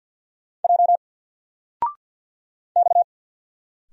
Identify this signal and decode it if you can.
Morse code, heard as a keyed tone, about 48 wpm, 700 Hz. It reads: J X
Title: Single Character (Letter:Number:Punctuation:Prosign) 50wpm